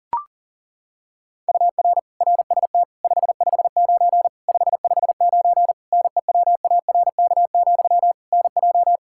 Morse code, heard as a keyed tone, about 37 wpm, 700 Hz. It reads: UR RST 559 559 NEWARK, NJ
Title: Practice QSO - Exchange 1 - Beginning - 40wpm